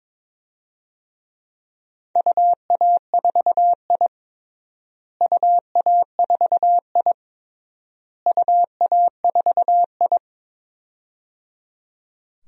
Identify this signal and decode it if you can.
Morse code, heard as a keyed tone, about 22 wpm, 700 Hz. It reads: UA4I UA4I UA4I
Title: DXCC-100 Call Signs - 4 Characters - Repeated 3x - 22wpm